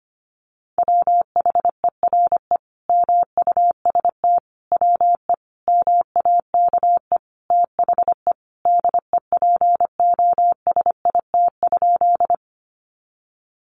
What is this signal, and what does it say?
Morse code, heard as a keyed tone, about 25 wpm, 700 Hz. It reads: WHERE MUST WE MAKE THE DEPOSIT?
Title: Sentences from Top 1300 Words 25wpm